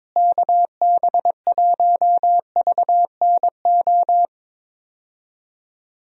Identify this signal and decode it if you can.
Morse code, heard as a keyed tone, about 22 wpm, 700 Hz. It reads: KB1VNO
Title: USA Call Signs - 6 Characters - Repeated 3x - 22wpm